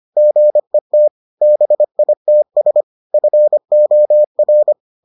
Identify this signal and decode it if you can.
Morse code, heard as a keyed tone, about 25 wpm, 600 Hz. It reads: GET BITS FOR